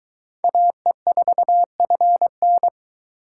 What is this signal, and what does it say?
Morse code, heard as a keyed tone, about 20 wpm, 700 Hz. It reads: AE4FN